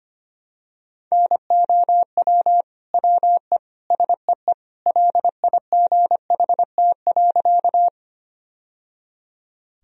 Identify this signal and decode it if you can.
Morse code, heard as a keyed tone, about 25 wpm, 700 Hz. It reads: NOW WE SEE LIGHT.
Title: Sentences from Top 1000 Words - Review Part 1 - 25wpm